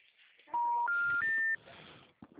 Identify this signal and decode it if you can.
Special information tone.